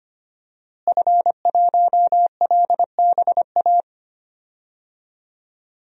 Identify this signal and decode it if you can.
Morse code, heard as a keyed tone, about 25 wpm, 700 Hz. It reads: F1LBA
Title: DXCC-100 Call Signs - 5 Characters - Repeated 2x - 25wpm